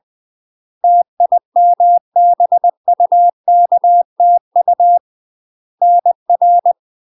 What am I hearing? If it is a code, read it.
Morse code, heard as a keyed tone, about 20 wpm, 700 Hz. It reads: TIMBUKTU NR
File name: UK MASTER QSOs TRAINING / 20WPM / UK MASTER QSOs 0032